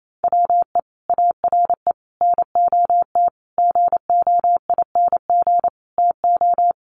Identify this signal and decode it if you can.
Morse code, heard as a keyed tone, about 28 wpm, 700 Hz. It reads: WE ARE NOT GOING TO
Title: Sets of 5 Words 28wpm